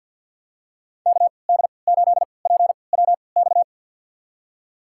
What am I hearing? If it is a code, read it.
Morse code, heard as a keyed tone, about 46 wpm, spaced as for 27 wpm, 700 Hz. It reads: KD8PWX
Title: USA Call Signs - 3x Character Spacing - 50wpm